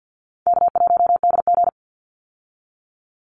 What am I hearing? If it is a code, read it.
Morse code, heard as a keyed tone, about 50 wpm, 700 Hz. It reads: K1DG
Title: Easy Call Signs Repeated 2x - 50wpm